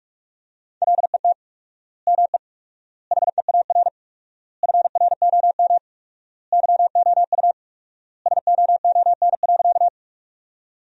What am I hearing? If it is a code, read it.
Morse code, heard as a keyed tone, about 42 wpm, 700 Hz. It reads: LET ME HEAR FROM YOU SOON.
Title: 4x Word-Spacing - Sentences from Top 400 Words - 45wpm